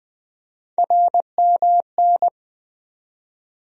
Morse code, heard as a keyed tone, about 21 wpm, 700 Hz. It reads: RMN
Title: USA Call Sign Postfixes - 3 Letters 20wpm